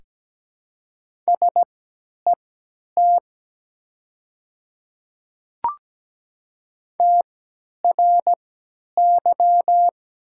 Morse code, heard as a keyed tone, about 17 wpm, spaced as for 10 wpm, 700 Hz. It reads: SET TRY
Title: 3x Character Spacing - Top 300 Words - 17wpm